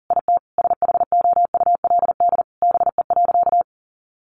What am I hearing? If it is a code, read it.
Morse code, heard as a keyed tone, about 40 wpm, 700 Hz. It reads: IT SHOULD BE.
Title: Sentences from Top 900 Words 40wpm